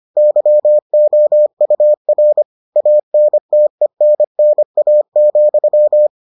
Morse code, heard as a keyed tone, about 25 wpm, 600 Hz. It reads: YOUR ANTENNA,